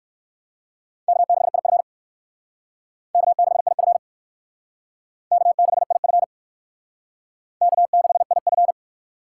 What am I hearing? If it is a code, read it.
Morse code, heard as a keyed tone, about 56 wpm, 700 Hz. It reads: K6IF K6IF K6IF K6IF